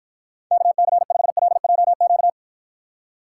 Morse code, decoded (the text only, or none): KC5LJQ